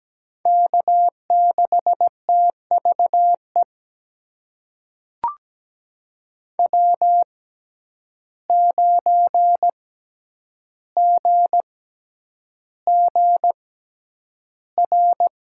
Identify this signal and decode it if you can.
Morse code, heard as a keyed tone, about 17 wpm, spaced as for 9 wpm, 700 Hz. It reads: K6TVE W9GGR
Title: USA Call Signs - 5 Characters - 6x Character Spacing - 17wpm